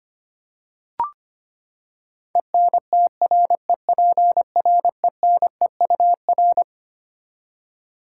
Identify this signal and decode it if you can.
Morse code, heard as a keyed tone, about 25 wpm, 700 Hz. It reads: ENTREPRENEUR
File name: Twelve Letter Words - 25wpm